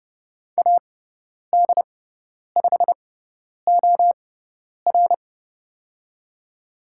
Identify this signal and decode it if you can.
Morse code, heard as a keyed tone, about 30 wpm, spaced as for 10 wpm, 700 Hz. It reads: AD5OR
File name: Lesson 16g - Letter c - Call Signs 30wpm